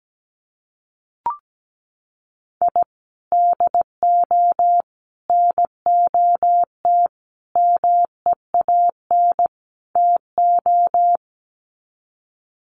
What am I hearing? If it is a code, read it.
Morse code, heard as a keyed tone, about 17 wpm, 700 Hz. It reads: I DO NOT MEAN TO
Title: Sets of 5 Words 17wpm